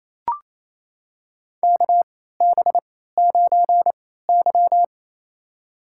Morse code, heard as a keyed tone, about 28 wpm, spaced as for 16 wpm, 700 Hz. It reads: KB9Y
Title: USA Call Signs - 4 Characters - 3x Character Spacing - 28wpm